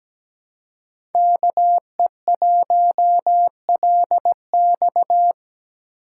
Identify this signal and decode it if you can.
Morse code, heard as a keyed tone, about 17 wpm, 700 Hz. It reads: KE1LX
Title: USA Call Signs - 5 Characters - 3x Character Spacing - 17wpm